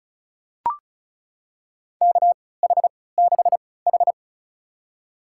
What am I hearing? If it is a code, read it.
Morse code, heard as a keyed tone, about 36 wpm, spaced as for 20 wpm, 700 Hz. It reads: KH6H
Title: USA Call Signs - 4 Characters - 3x Character Spacing - 35wpm